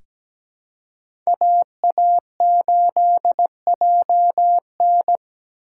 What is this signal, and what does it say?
Morse code, heard as a keyed tone, about 17 wpm, 700 Hz. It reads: AA8JN
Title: USA Call Signs - 5 Characters - 4x Character Spacing - 17wpm